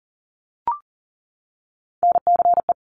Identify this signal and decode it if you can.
Morse code, heard as a keyed tone, about 40 wpm, 700 Hz. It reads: NCE
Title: Top Three Letter Combinations in Words 40wpm